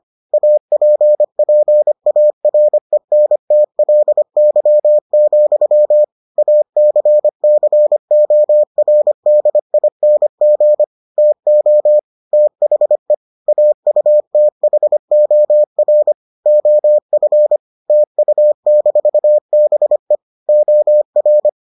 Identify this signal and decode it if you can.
Morse code, heard as a keyed tone, about 25 wpm, 600 Hz. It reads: APPARENTLY, ACCORDING TO THE AUTHOR OF TU-BE OR